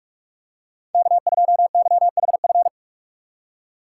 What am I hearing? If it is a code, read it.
Morse code, heard as a keyed tone, about 45 wpm, 700 Hz. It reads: K2YHF
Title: USA Call Signs - 5 Characters - 3x Character Spacing - 45wpm